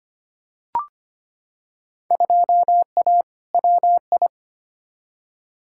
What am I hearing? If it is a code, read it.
Morse code, heard as a keyed tone, about 25 wpm, 700 Hz. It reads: 2A WI
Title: Field Day Exchanges - 25wpm